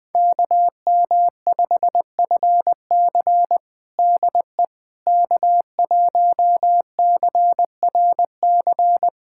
Morse code, heard as a keyed tone, about 20 wpm, 700 Hz. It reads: KM5FC DE K1CRC
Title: USA Callsign DE Callsign - 5 Characters - 20wpm